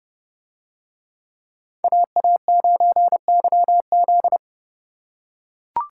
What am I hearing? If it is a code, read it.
Morse code, heard as a keyed tone, about 30 wpm, 700 Hz. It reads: AA9YZ